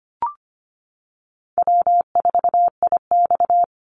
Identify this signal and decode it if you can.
Morse code, heard as a keyed tone, about 25 wpm, 700 Hz. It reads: W4IX
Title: Easy Call Signs Repeated 3x - 25wpm